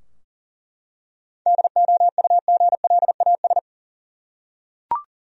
Morse code, heard as a keyed tone, about 40 wpm, 700 Hz. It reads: DOUGLAS
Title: USA Common Names - 40wpm